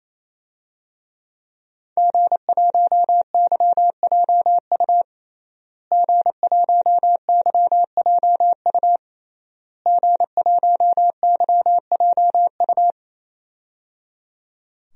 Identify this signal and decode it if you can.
Morse code, heard as a keyed tone, about 28 wpm, 690 Hz. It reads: G1YJU G1YJU G1YJU
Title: DXCC-100 Call Signs - 5 Characters - Repeated 3x - 28wpm